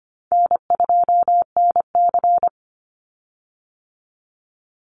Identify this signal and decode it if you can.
Morse code, heard as a keyed tone, about 25 wpm, 700 Hz. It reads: N2NC
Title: Easy Call Signs Repeated 3x - 25wpm